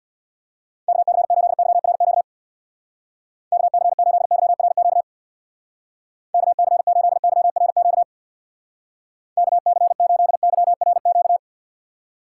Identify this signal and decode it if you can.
Morse code, heard as a keyed tone, about 65 wpm, 700 Hz. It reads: KC8YRQ KC8YRQ KC8YRQ KC8YRQ